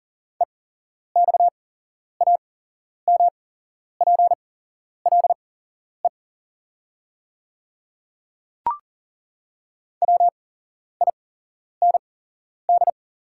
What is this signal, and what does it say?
Morse code, heard as a keyed tone, about 40 wpm, spaced as for 11 wpm, 690 Hz. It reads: EXAMPLE WIND
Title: 8x Character Spacing - Top 500 Words - 40wpm